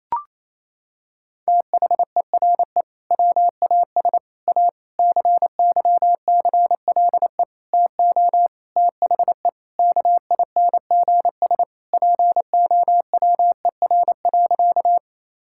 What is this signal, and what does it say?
Morse code, heard as a keyed tone, about 28 wpm, 700 Hz. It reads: THERE WAS A CYCLE TO THE KINGS POWER.
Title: Sentences from Top 1000 Words - Review Part 6 - 28wpm